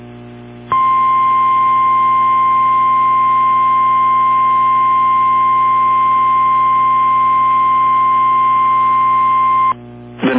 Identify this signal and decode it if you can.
Emergency Alert System (SAME) tones (the 1050 Hz attention tone). It is heard from 0.7 to 9.8 s.